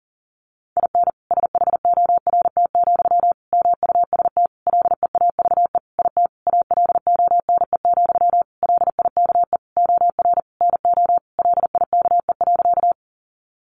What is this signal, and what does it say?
Morse code, heard as a keyed tone, about 40 wpm, 700 Hz. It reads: IN SHORT, MUST LEAVE IT ALONE, LIKE OR NO LIKE.
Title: Sentences from Top 1000 Words 40wpm